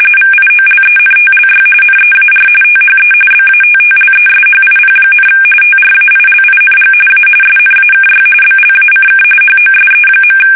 RTTY, about 45.45 baud. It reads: VIOLENTLY NEAR THE BATTLESHIP, TRAILING A SNAKY BLACK HOSE.
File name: fsk850 / battleshipnj